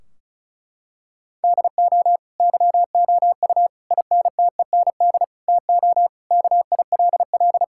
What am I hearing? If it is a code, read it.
Morse code, heard as a keyed tone, about 36 wpm, 700 Hz. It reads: DO YOU INTEND TO KILL
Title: Sentences from Top 1600 Words 35wpm